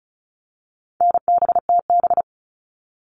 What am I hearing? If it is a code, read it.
Morse code, heard as a keyed tone, about 34 wpm, 700 Hz. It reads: NBTB
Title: All Cut Numbers - 4-Digit 35wpm